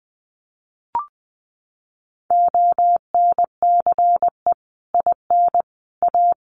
Morse code, heard as a keyed tone, about 20 wpm, 700 Hz. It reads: ONCE IN A
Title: Sets of 4 Words 20wpm